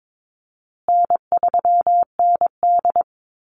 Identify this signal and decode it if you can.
Morse code, heard as a keyed tone, about 22 wpm, 700 Hz. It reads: N3ND
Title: Easy Call Signs Repeated 3x - 22wpm